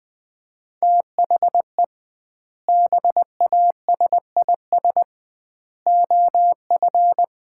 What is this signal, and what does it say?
Morse code, heard as a keyed tone, about 21 wpm, 700 Hz. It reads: THE BASIS OF
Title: 2x Word-Spacing - Sets of 3 Words - 20wpm